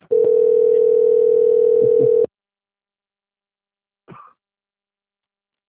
Ringback tone (North America), 440 + 480 Hz.
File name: out- - - - - .88246